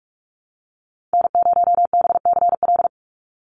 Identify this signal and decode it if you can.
Morse code, heard as a keyed tone, about 44 wpm, 700 Hz. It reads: N0BCL
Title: USA Call Signs 45wpm